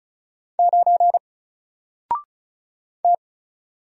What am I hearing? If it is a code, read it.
Morse code, heard as a keyed tone, about 34 wpm, 700 Hz. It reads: 9 T
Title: Single Letter-Number 35wpm